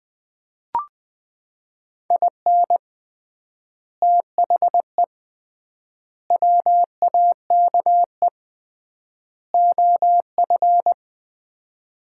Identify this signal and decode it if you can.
Morse code, heard as a keyed tone, about 20 wpm, 700 Hz. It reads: IN THE WAKE OF
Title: 3x Word-Spacing - Sets of 4 Words - 20wpm